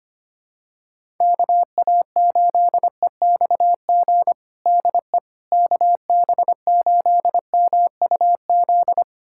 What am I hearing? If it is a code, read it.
Morse code, heard as a keyed tone, about 25 wpm, 700 Hz. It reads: KA8EXG DE KB8MUZ
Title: USA Callsign DE Callsign 25wpm